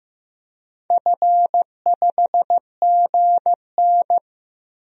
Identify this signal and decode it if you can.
Morse code, heard as a keyed tone, about 15 wpm, 700 Hz. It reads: F5GN
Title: DXCC-100 Call Signs - 4 Characters - Repeated 3x - 15wpm